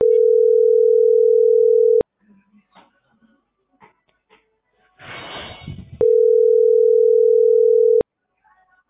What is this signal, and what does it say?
Ringback tone (North America), 440 + 480 Hz.